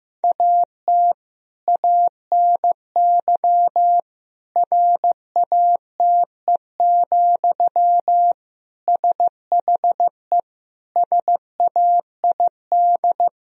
Morse code, heard as a keyed tone, about 15 wpm, 700 Hz. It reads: AT ANY RATE, SHE SAID
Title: Sentences from Top 1000 Words - Review Part 4 - 15wpm